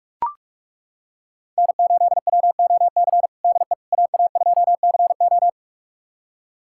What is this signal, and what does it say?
Morse code, heard as a keyed tone, about 41 wpm, 700 Hz. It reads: N8WOC DE AA2CO
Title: USA Callsign DE Callsign - 5 Characters - 45wpm